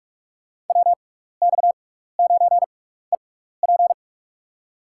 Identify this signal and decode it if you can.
Morse code, heard as a keyed tone, about 44 wpm, spaced as for 15 wpm, 700 Hz. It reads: WX9EP
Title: USA Call Signs - 5 Characters - 6x Character Spacing - 45wpm